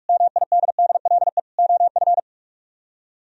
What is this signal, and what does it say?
Morse code, heard as a keyed tone, about 56 wpm, 700 Hz. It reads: MIDDLE OF
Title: Speed-Racing - Sets of 3 Words 45wpm